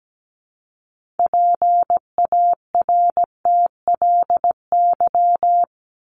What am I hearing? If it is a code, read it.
Morse code, heard as a keyed tone, about 17 wpm, 700 Hz. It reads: PARTLY
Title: LY Suffix - Top 100 Words with Base Words 17wpm